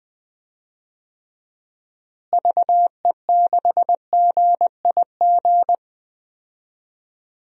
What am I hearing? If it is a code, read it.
Morse code, heard as a keyed tone, about 21 wpm, 700 Hz. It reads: VE6GIG
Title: DXCC-100 Call Signs - 6 Characters - Repeated 3x - 20wpm